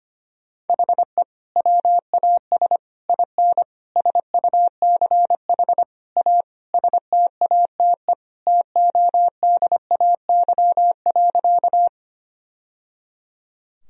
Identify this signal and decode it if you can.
Morse code, heard as a keyed tone, about 25 wpm, 700 Hz. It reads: HE WAS IN SUCH A STATE TODAY.
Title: Sentences from Top 1000 Words - Review Part 2 - 25wpm